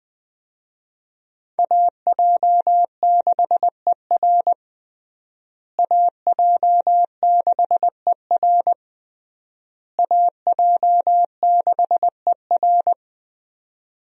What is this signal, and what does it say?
Morse code, heard as a keyed tone, about 20 wpm, 700 Hz. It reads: AJ6ER AJ6ER AJ6ER
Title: USA Call Signs - 5 Characters - Repeated 3x - 20wpm